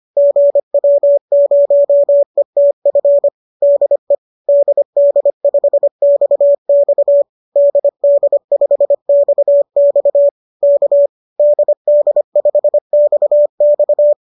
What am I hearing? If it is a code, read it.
Morse code, heard as a keyed tone, about 25 wpm, 590 Hz. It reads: GW0ETF DE DD5XX DD5XX K DD5XX
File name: qso218 25